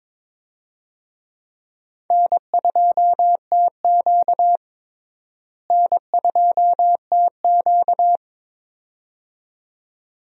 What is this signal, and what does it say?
Morse code, heard as a keyed tone, about 22 wpm, 700 Hz. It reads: N2TQ N2TQ
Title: USA Call Signs - 4 Characters - Repeated 2x - 22wpm